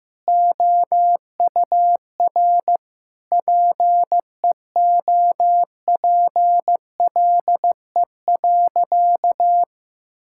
Morse code, heard as a keyed tone, about 15 wpm, 700 Hz. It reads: OUR PEOPLE.
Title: Sentences from Top 1000 Words - Review Part 6 - 15wpm